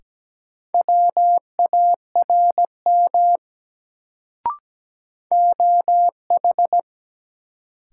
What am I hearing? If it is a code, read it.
Morse code, heard as a keyed tone, about 17 wpm, 700 Hz. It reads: WARM OH
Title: Sentences from Top 500 Words - New Word Drill - 17wpm